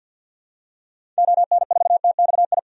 Morse code, heard as a keyed tone, about 49 wpm, 700 Hz. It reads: ON4TXI
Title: DXCC-100 Call Signs - 6 Characters - 50wpm